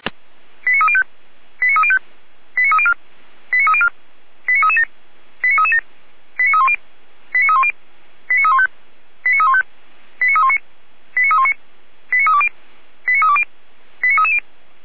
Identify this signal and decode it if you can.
Five-tone selective call (ZVEI), heard as quick runs of five tones, tones of about 80 ms. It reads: 89305 89305 89304 89304 89337 89337 89310 89310 89315 89315 89319 89319 89320 89320 89330